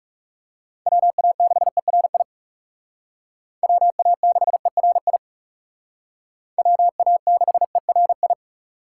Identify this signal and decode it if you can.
Morse code, heard as a keyed tone, about 36 wpm, 700 Hz. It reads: WA6ERI WA6ERI WA6ERI
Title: Speed Racing - USA Call Signs 30wpm